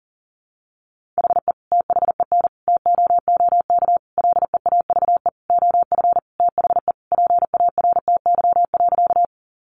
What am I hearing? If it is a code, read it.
Morse code, heard as a keyed tone, about 40 wpm, 700 Hz. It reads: HE THEN TOOK LEAVE OF THE PARTY.
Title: Sentences from Top 1400 Words 40wpm